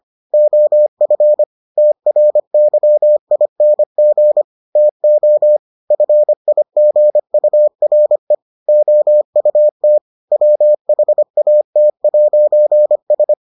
Morse code, heard as a keyed tone, about 25 wpm, 600 Hz. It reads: OF TRYING TO FIGURE OUT WHAT'S